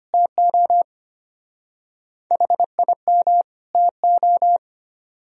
Morse code, heard as a keyed tone, about 30 wpm, 700 Hz. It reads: TO HIM TO
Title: Speed-Racing - Sets of 2 Words 25wpm